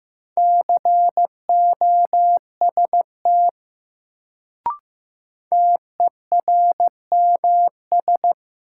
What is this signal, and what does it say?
Morse code, heard as a keyed tone, about 15 wpm, 700 Hz. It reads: COST TERMS
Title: Sets of 3 Words - New Word Drill - 15wpm